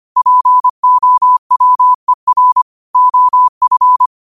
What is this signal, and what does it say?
Morse code, heard as a keyed tone, about 25 wpm, 1000 Hz. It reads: POWER OF
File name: qotdfiles / 25wpm